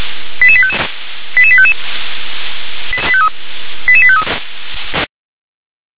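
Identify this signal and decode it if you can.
Five-tone selective call (ZVEI), heard as quick runs of five tones, tones of about 70 ms. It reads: 88855 88855 88853
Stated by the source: ZVEI five-tone selective call received by SDR.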